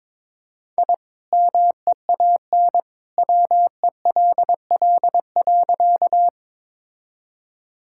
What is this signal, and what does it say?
Morse code, heard as a keyed tone, about 22 wpm, 700 Hz. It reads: I MEAN WELL.
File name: practice / 22wpm / Sentences with 2-Word Phrases - 22wpm